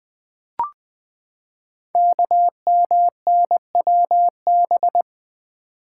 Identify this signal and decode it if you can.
Morse code, heard as a keyed tone, about 20 wpm, 700 Hz. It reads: KMNWB
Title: Code Groups - 5 Letters - 20wpm